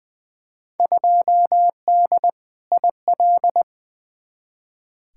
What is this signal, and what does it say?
Morse code, heard as a keyed tone, about 20 wpm, 700 Hz. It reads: 2D IL
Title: Field Day Exchanges - 20wpm